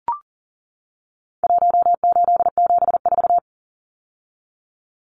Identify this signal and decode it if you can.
Morse code, heard as a keyed tone, about 40 wpm, 700 Hz. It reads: 1874
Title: Numbers - Contest Style - 4-Digit 40wpm